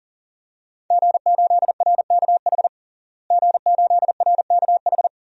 Morse code, heard as a keyed tone, about 37 wpm, 690 Hz. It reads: G8RKH G8RKH
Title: DXCC-100 Call Signs - 5 Characters - Repeated 2x - 40wpm